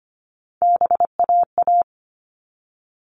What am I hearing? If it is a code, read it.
Morse code, heard as a keyed tone, about 25 wpm, 700 Hz. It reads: BAA
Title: All Cut Numbers - 3-Digit 25wpm